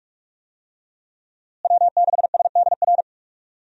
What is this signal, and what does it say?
Morse code, heard as a keyed tone, about 42 wpm, 700 Hz. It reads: W6SDR
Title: USA Call Signs - 5 Characters - Repeated 2x - 45wpm